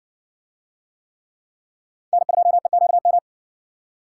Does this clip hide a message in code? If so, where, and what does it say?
Yes: Morse code from 2.1 to 3.2 s, heard as a keyed tone, about 60 wpm, 700 Hz. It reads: N2EQM